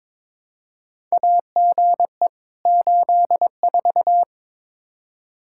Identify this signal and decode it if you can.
Morse code, heard as a keyed tone, about 21 wpm, 700 Hz. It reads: AGE 84
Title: Practice QSO - Exchange 3 - More Info - 22wpm